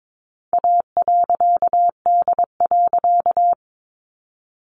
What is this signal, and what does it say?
Morse code, heard as a keyed tone, about 22 wpm, 700 Hz. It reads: A.D.
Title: A Sound Of Thunder 22wpm